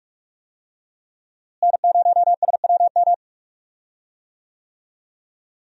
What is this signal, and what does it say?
Morse code, heard as a keyed tone, about 46 wpm, 700 Hz. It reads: N0SWM
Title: USA Call Signs - 5 Characters - Repeated 2x - 45wpm